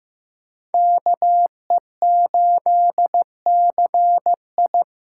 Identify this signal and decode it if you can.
Morse code, heard as a keyed tone, about 15 wpm, 700 Hz. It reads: KE8CI
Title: USA Call Signs - 8x Character Spacing - 15wpm